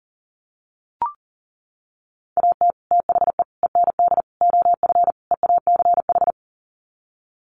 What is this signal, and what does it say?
Morse code, heard as a keyed tone, about 40 wpm, 700 Hz. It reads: AT THE END OF EACH
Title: Sets of 5 Words Encore 40wpm